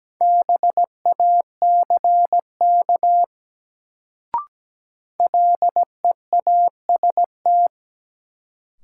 Morse code, heard as a keyed tone, about 17 wpm, 700 Hz. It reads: BACK LEAST
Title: Sets of 2 Words - New Word Drill - 17wpm